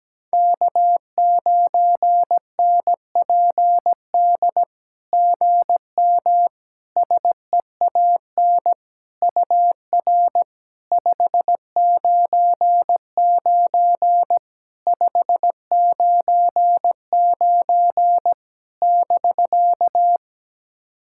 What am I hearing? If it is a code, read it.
Morse code, heard as a keyed tone, about 17 wpm, 700 Hz. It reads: K9NPD GM SEAN UR 599 599 <BK>
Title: SOTA QSO Practice 17wpm